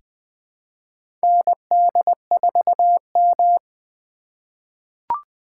Morse code, heard as a keyed tone, about 21 wpm, 700 Hz. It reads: ND4M
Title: USA Call Signs - 4 Characters - 3x Character Spacing - 20wpm